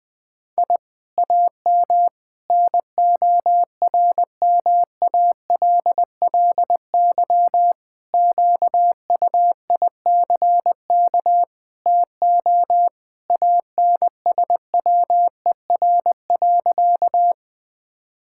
Morse code, heard as a keyed tone, about 20 wpm, 700 Hz. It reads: I AM NORMALLY QUICK TO ANSWER.
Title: Sentences from Top 1000 Words - Review Part 6 - 20wpm